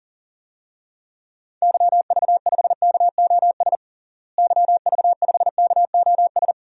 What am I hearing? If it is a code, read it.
Morse code, heard as a keyed tone, about 40 wpm, 700 Hz. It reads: YV5KOS YV5KOS
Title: DXCC-100 Call Signs - 6 Characters - Repeated 2x - 40wpm